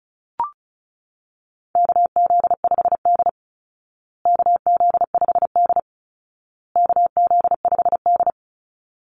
Morse code, heard as a keyed tone, about 34 wpm, 700 Hz. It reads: KZ5D KZ5D KZ5D
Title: Call Signs Repeated 2x - 35wpm